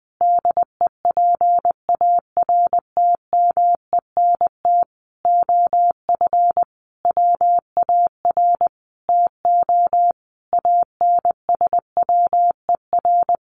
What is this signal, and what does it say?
Morse code, heard as a keyed tone, about 20 wpm, 700 Hz. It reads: DEPARTMENT OF WAR TO ANSWER
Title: Sentences from Top 1500 Words 20wpm